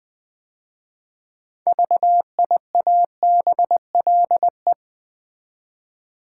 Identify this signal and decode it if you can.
Morse code, heard as a keyed tone, about 20 wpm, 700 Hz. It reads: VIABLE
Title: 6 Letter Words - First 3 Letters Given - 20wpm